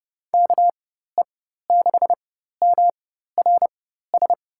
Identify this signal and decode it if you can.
Morse code, heard as a keyed tone, about 30 wpm, spaced as for 14 wpm, 700 Hz. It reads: KE6MRS
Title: USA Call Signs - 4x Character Spacing - 30wpm